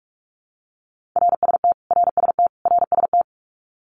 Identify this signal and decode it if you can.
Morse code, heard as a keyed tone, about 46 wpm, 700 Hz. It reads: RST RST RST
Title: Abbreviations Repeated 3x - 45wpm